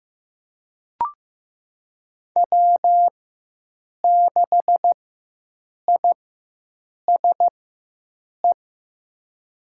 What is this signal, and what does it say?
Morse code, heard as a keyed tone, about 15 wpm, spaced as for 7 wpm, 700 Hz. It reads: W6ISE
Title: USA Call Signs - 5 Characters - 4x Character Spacing - 15wpm